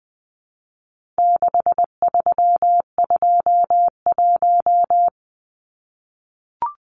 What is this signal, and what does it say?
Morse code, heard as a keyed tone, about 20 wpm, 700 Hz. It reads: 6321
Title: Numbers - 4-Digit 20wpm